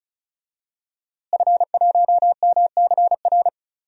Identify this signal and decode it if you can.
Morse code, heard as a keyed tone, about 36 wpm, 700 Hz. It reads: F1MCR
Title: DXCC-100 Call Signs - 5 Characters - Repeated 2x - 35wpm